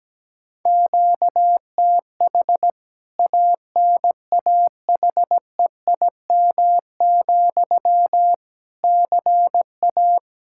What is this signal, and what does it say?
Morse code, heard as a keyed tone, about 17 wpm, 700 Hz. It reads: QTH ANAHEIM, CA
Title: K1USN SST QSO Practice 17wpm